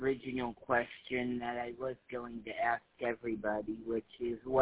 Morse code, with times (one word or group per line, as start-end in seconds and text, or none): none